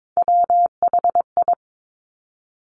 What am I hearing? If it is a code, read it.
Morse code, heard as a keyed tone, about 22 wpm, 700 Hz. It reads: WHI
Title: Top Three Letter Combinations in Words 22wpm